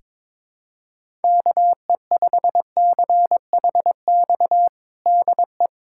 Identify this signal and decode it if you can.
Morse code, heard as a keyed tone, about 22 wpm, 700 Hz. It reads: KE5CHX DE
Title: USA Callsign DE Callsign - 6 Characters - 22wpm